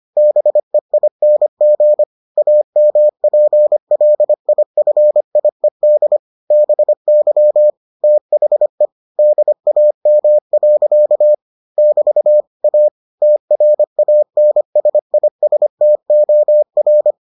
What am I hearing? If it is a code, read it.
Morse code, heard as a keyed tone, about 25 wpm, 600 Hz. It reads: BEING AMPLIFIED BY THE DAM. <BT> A TRANSISTOR